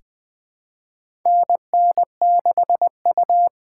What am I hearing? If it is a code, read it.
Morse code, heard as a keyed tone, about 20 wpm, 700 Hz. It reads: NN6U
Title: USA Call Signs - 4 Characters - 4x Character Spacing - 20wpm